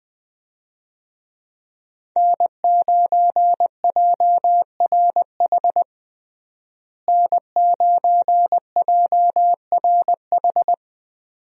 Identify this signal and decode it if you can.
Morse code, heard as a keyed tone, about 20 wpm, 700 Hz. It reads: N9JRH N9JRH
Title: USA Call Signs - 5 Characters - Repeated 2x - 20wpm